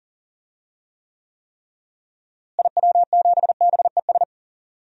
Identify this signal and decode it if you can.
Morse code, heard as a keyed tone, about 39 wpm, 700 Hz. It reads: IW7BES
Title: DXCC-100 Call Signs - 6 Characters - Repeated 3x - 40wpm